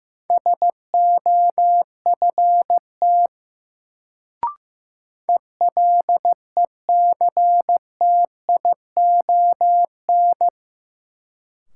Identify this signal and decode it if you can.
Morse code, heard as a keyed tone, about 15 wpm, 700 Hz. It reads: SOFT ELECTION
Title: Sentences from Top 1500 Words - New Word Drill Alt - 15wpm